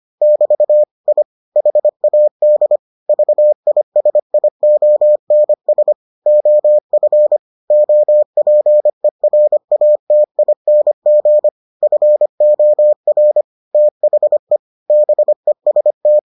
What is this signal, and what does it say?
Morse code, heard as a keyed tone, about 25 wpm, 600 Hz. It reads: <BT> I HAD VISIONS OF OPERATING FOR THE BEST